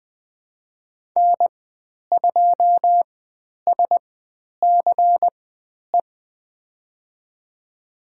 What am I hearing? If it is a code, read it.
Morse code, heard as a keyed tone, about 20 wpm, spaced as for 10 wpm, 700 Hz. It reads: N2SCE
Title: 095 Lesson 16g - Letter c - Call Signs 20wpm